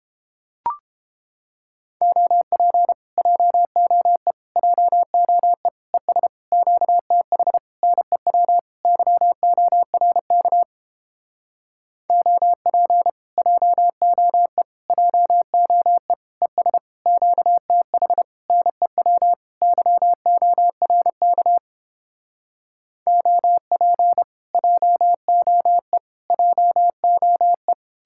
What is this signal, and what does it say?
Morse code, heard as a keyed tone, about 29 wpm, 700 Hz. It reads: OP JOE JOE ES QTH NEW YORK OP JOE JOE ES QTH NEW YORK OP JOE JOE